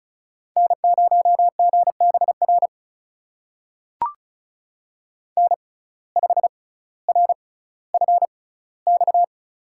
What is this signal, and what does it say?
Morse code, heard as a keyed tone, about 36 wpm, spaced as for 18 wpm, 700 Hz. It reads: N0GBR N5RFX
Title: USA Call Signs - 5 Characters - 6x Character Spacing - 35wpm